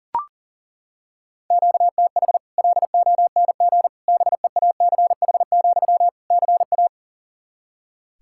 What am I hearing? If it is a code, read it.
Morse code, heard as a keyed tone, about 40 wpm, 700 Hz. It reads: QTH LONG BEACH, CA
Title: Practice QSO - Exchange 1 - Beginning - 40wpm